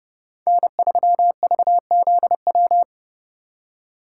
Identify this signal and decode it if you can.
Morse code, heard as a keyed tone, about 32 wpm, 700 Hz. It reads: N3VZW
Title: USA Call Signs - 2x Character Spacing - 30wpm